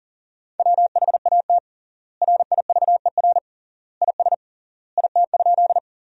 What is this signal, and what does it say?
Morse code, heard as a keyed tone, about 37 wpm, 700 Hz. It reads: WHAT RIVER IS IT?
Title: 3x Word-Spacing - Sentences from Top 300 Words - 40wpm